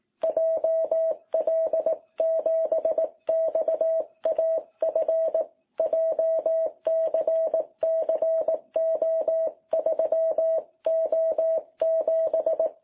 Morse code, heard as a keyed tone, about 18 wpm, 660 Hz. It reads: JL7XAF JCCO3O7
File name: jh8cbh / 2021cw